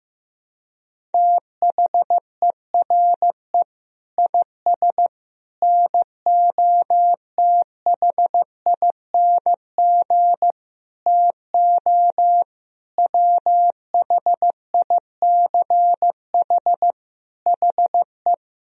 Morse code, heard as a keyed tone, about 15 wpm, 700 Hz. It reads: THERE IS NOTHING TO WHICH HE
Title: Sentences from Top 1000 Words - Review Part 3 - 15wpm